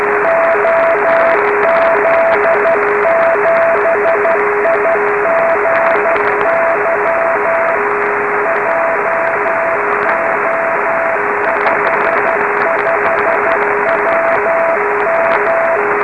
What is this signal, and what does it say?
Morse code, heard as a keyed tone, about 12 wpm, 660 Hz, with radio noise. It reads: OZ7IGY JO55WM